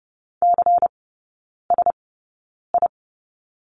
Morse code, heard as a keyed tone, about 30 wpm, spaced as for 9 wpm, 700 Hz. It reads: CSI
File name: Lesson 16e - Letter c - Three Characters 30wpm